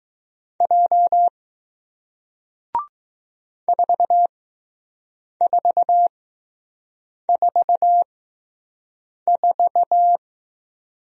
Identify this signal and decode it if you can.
Morse code, heard as a keyed tone, about 20 wpm, spaced as for 6 wpm, 700 Hz. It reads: J 4444